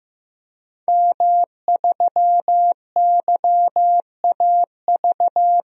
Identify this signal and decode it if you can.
Morse code, heard as a keyed tone, about 15 wpm, 700 Hz. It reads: M3YAV
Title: DXCC-100 Call Signs - 5 Characters - 15wpm